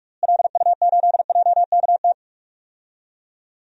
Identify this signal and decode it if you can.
Morse code, heard as a keyed tone, about 42 wpm, 700 Hz. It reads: LU8JKT
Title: DXCC-100 Call Signs - 6 Characters - Repeated 3x - 45wpm